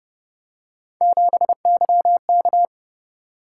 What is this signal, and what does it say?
Morse code, heard as a keyed tone, about 29 wpm, 700 Hz. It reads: 7YK